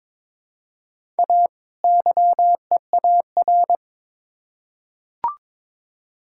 Morse code, heard as a keyed tone, about 22 wpm, 700 Hz. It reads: A YEAR
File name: Build A Phrase - 2-Word Phrases - 22wpm